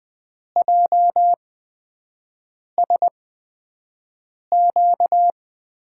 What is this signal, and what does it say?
Morse code, heard as a keyed tone, about 20 wpm, spaced as for 5 wpm, 700 Hz. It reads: JSQ
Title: USA Call Signs - 8x Character Spacing - 20wpm